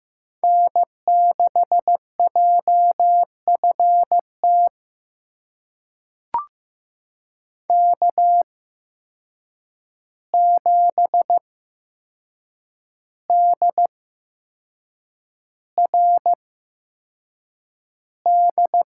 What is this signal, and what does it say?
Morse code, heard as a keyed tone, about 15 wpm, spaced as for 6 wpm, 700 Hz. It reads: N6JFT K7DRD
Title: USA Call Signs - 5 Characters - 8x Character Spacing - 15wpm